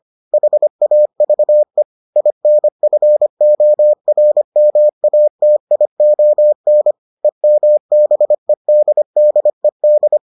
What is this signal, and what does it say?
Morse code, heard as a keyed tone, about 25 wpm, 600 Hz. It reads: HAVE INFORMATION EMBEDDED